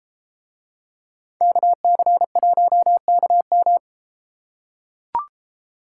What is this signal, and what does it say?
Morse code, heard as a keyed tone, about 32 wpm, 700 Hz. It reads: KC1KM